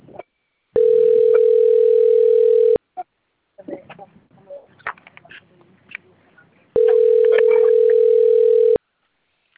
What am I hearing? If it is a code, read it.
Ringback tone (North America), 440 + 480 Hz.